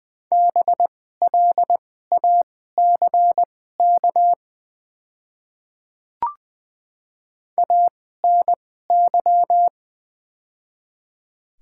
Morse code, heard as a keyed tone, about 20 wpm, spaced as for 14 wpm, 700 Hz. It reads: BLACK ANY
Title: 2x Character Spacing - Top 500 Words - 20wpm